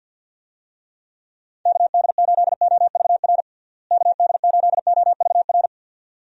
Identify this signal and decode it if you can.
Morse code, heard as a keyed tone, about 46 wpm, 700 Hz. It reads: KD8OVR KD8OVR
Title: USA Call Signs - 6 Characters - Repeated 2x - 50wpm